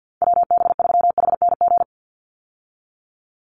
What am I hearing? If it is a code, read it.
Morse code, heard as a keyed tone, about 50 wpm, 700 Hz. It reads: WB3HNG
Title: USA Call Signs 50wpm